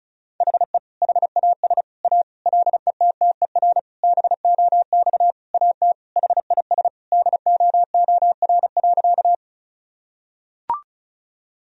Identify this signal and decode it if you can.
Morse code, heard as a keyed tone, about 34 wpm, 700 Hz. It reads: HE HAS A LETTER BOX AT HIS DOOR.
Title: Sentences from Top 1000 Words - Review Part 2 - 35wpm